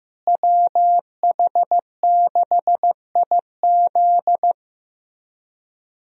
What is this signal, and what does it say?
Morse code, heard as a keyed tone, about 15 wpm, 700 Hz. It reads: WH6IZ
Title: USA Call Signs - 5 Characters - Repeated 2x - 15wpm